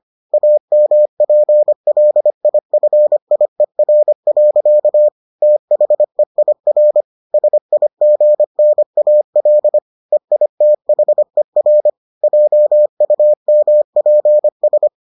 Morse code, heard as a keyed tone, about 25 wpm, 600 Hz. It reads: AMPLIFIER. THEIR SIGNAL EITHER JUMPS